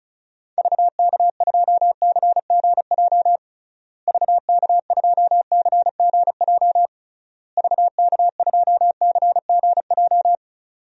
Morse code, heard as a keyed tone, about 34 wpm, 700 Hz. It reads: VK2CGJ VK2CGJ VK2CGJ
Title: DXCC-100 Call Signs - 6 Characters - Repeated 3x - 35wpm